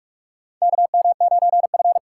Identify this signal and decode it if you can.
Morse code, heard as a keyed tone, about 45 wpm, 700 Hz. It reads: KM9F
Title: USA Call Signs - 4 Characters - 6x Character Spacing - 45wpm